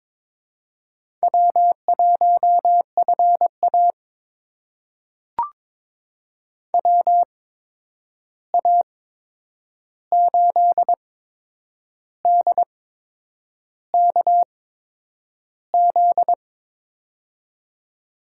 Morse code, heard as a keyed tone, about 23 wpm, spaced as for 8 wpm, 700 Hz. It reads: W1FA WA8DKZ
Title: USA Call Signs - 8x Character Spacing - 22wpm